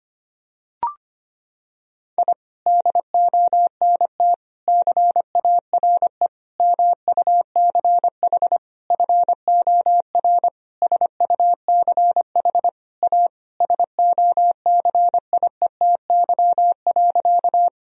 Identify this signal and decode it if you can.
Morse code, heard as a keyed tone, about 25 wpm, 700 Hz. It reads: I DONT CARE MUCH FOR SUCH A SOCIETY.
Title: Sentences from Top 1000 Words - Review Part 4 - 25wpm